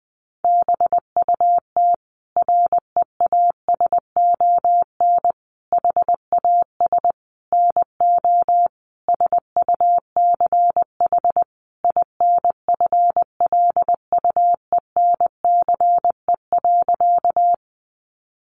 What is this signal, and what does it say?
Morse code, heard as a keyed tone, about 20 wpm, 700 Hz. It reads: BUT REASON HAS NO SUCH INFLUENCE.
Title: Sentences from Top 1100 Words 20wpm